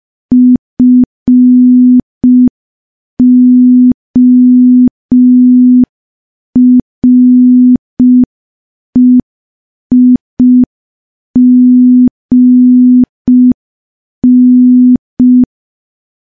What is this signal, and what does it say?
Morse code, heard as a keyed tone, about 5 wpm, 250 Hz. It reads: FOREIGN
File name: morse / morse-training-05WPM